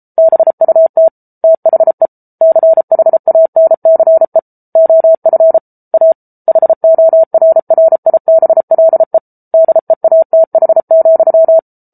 Morse code, heard as a keyed tone, about 34 wpm, 650 Hz. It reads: BUT THE CHANCE OF A HORRIBLE DEATH,